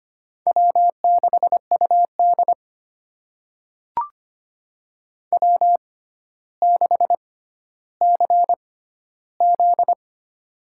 Morse code, heard as a keyed tone, about 25 wpm, spaced as for 13 wpm, 700 Hz. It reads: W6UD W6CZ
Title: USA Call Signs - 6x Character Spacing - 25wpm